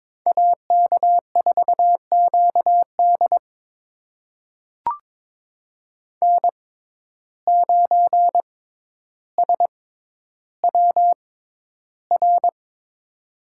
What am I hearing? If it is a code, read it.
Morse code, heard as a keyed tone, about 22 wpm, spaced as for 11 wpm, 700 Hz. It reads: AK4QD N9SWR
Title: USA Call Signs - 5 Characters - 6x Character Spacing - 22wpm